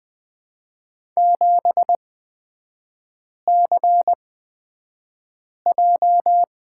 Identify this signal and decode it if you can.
Morse code, heard as a keyed tone, about 20 wpm, spaced as for 5 wpm, 700 Hz. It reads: 7CJ
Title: 229 Lesson 37e - Letter x - Three Characters 20wpm